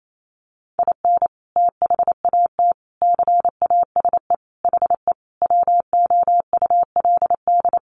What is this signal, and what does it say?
Morse code, heard as a keyed tone, about 28 wpm, 700 Hz. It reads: IN THAT CASE HE WOULD
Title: Sentences from Top 800 Words 28wpm